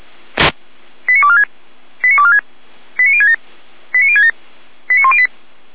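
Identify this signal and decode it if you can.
Five-tone selective call (ZVEI), heard as quick runs of five tones, tones of about 70 ms. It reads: 89246 89246 89067 89067 89108
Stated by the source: ZVEI five-tone selective call received by SDR.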